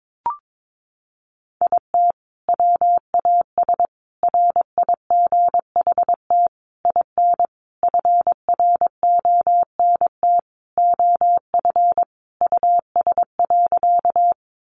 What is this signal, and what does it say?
Morse code, heard as a keyed tone, about 22 wpm, 700 Hz. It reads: IT WAS RIGHT IN FRONT OF US.
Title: Sentences from Top 500 Words 22wpm